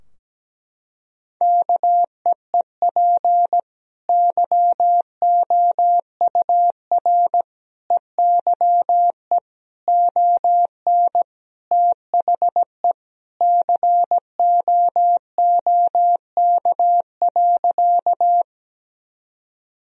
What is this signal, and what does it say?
Morse code, heard as a keyed tone, about 17 wpm, 700 Hz. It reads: KEEP YOUR EYE ON THE COOK.
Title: Sentences from Top 1000 Words - Review Part 4 - 17wpm